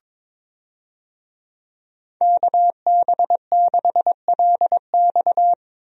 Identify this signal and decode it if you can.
Morse code, heard as a keyed tone, about 22 wpm, 700 Hz. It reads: KB6LX
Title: USA Call Signs - 5 Characters - Repeated 2x - 22wpm